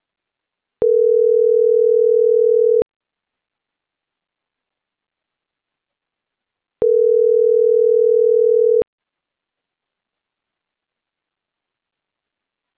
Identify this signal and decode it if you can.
Ringback tone (North America), 440 + 480 Hz.